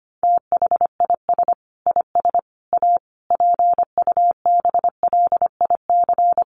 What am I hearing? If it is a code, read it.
Morse code, heard as a keyed tone, about 25 wpm, 700 Hz. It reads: THIS IS A PUBLIC
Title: Sentences from Top 600 Words 25wpm